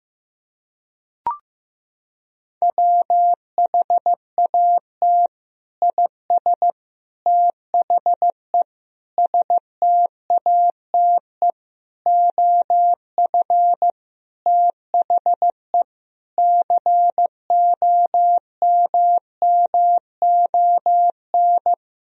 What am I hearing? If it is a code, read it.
Morse code, heard as a keyed tone, about 15 wpm, 700 Hz. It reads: WHAT IS THE STATE OF THE COMMON
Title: Sentences from Top 1000 Words - Review Part 3 - 15wpm